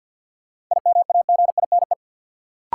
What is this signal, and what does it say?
Morse code, heard as a keyed tone, about 51 wpm, 700 Hz. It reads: IMAGINE
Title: Sentences from Top 1400 Words - New Word Drill - 50wpm